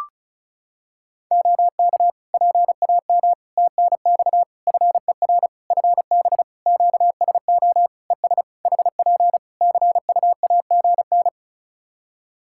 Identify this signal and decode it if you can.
Morse code, heard as a keyed tone, about 36 wpm, 700 Hz. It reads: OK PAM TNX FER FB QSO ES HP CUAGN
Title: K1USN SST QSO Practice 35wpm